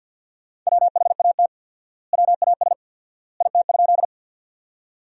Morse code, heard as a keyed tone, about 50 wpm, 700 Hz. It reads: WHAT WAS IT?
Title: 4x Word-Spacing - Sentences from Top 100 Words - 50wpm